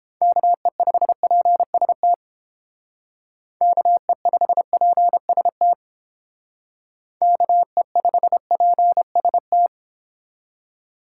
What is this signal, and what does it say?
Morse code, heard as a keyed tone, about 28 wpm, 700 Hz. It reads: KE5PST KE5PST KE5PST